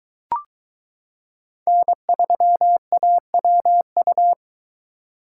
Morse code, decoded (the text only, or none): N3AWU